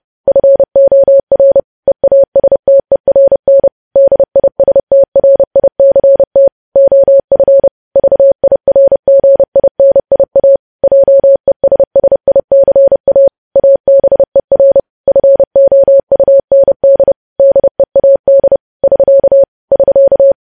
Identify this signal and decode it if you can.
Morse code, heard as a keyed tone, about 30 wpm, 550 Hz. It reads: FOR EASTERN DISTRICT OF VIRGINIA JESSICA ABER FOUND DEAD <SK> <SK>
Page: news headlines in morse code at 30 wpm